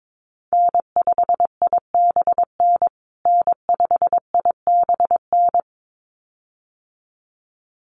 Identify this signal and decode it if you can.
Morse code, heard as a keyed tone, about 22 wpm, 700 Hz. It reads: N5IBN N5IBN
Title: USA Call Signs Repeated 2x 22wpm